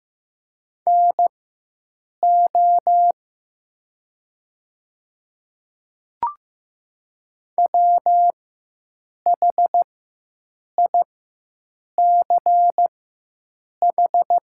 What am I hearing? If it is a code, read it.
Morse code, heard as a keyed tone, about 15 wpm, spaced as for 7 wpm, 700 Hz. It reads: NO WHICH
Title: 4x Character Spacing - Top 100 Words - 15wpm